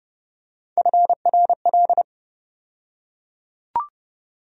Morse code, heard as a keyed tone, about 30 wpm, 700 Hz. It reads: FRL
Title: USA Call Sign Postfixes - 3 Letters 30wpm